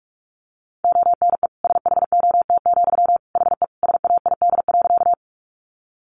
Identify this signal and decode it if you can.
Morse code, heard as a keyed tone, about 44 wpm, 700 Hz. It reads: ONE SHOT, HE SAID.
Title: Sentences from Top 1100 Words 45wpm